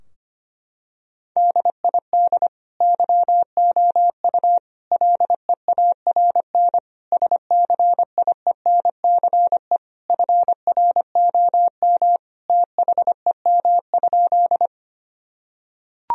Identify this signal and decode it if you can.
Morse code, heard as a keyed tone, about 25 wpm, 700 Hz. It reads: DID YOU LEARN SCIENCE FROM THEM?
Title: Sentences from Top 1000 Words - Review Part 2 - 25wpm